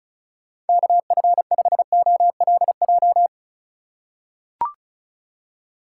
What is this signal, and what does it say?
Morse code, heard as a keyed tone, about 34 wpm, 700 Hz. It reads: KF5OLJ
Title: USA Call Signs - 6 Characters - 4x Character Spacing - 35wpm